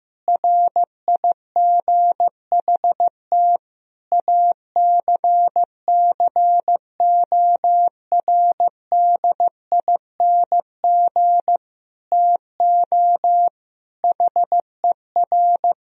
Morse code, heard as a keyed tone, about 15 wpm, 700 Hz. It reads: RIGHT ACCORDING TO HER
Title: Sentences from Top 1000 Words - Review Part 4 - 15wpm